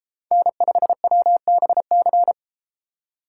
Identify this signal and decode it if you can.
Morse code, heard as a keyed tone, about 32 wpm, 700 Hz. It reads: N5WBC